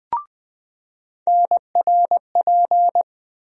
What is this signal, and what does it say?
Morse code, heard as a keyed tone, about 20 wpm, 700 Hz. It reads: NRP
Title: USA Call Sign Postfixes - 3 Letters 20wpm